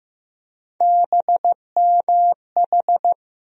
Morse code, heard as a keyed tone, about 15 wpm, 700 Hz. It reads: BMH
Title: USA Call Sign Postfixes - 3 Letters 15wpm